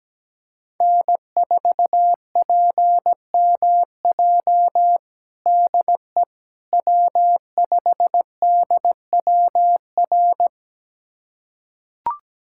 Morse code, heard as a keyed tone, about 17 wpm, 700 Hz. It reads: N4PMJ DE W5DWR
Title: USA Callsign DE Callsign - 5 Characters - 17wpm